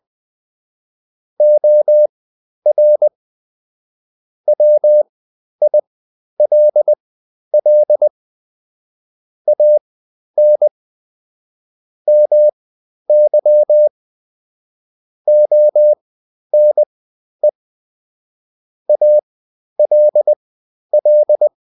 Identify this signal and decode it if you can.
Morse code, heard as a keyed tone, about 20 wpm, spaced as for 11 wpm, 600 Hz. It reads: OR WILL AN MY ONE ALL
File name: Top-100-Words-10